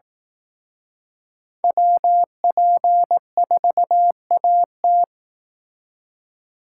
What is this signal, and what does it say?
Morse code, heard as a keyed tone, about 18 wpm, 700 Hz. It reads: WP4AT
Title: Speed Racing - USA 2x2 Call Signs 15wpm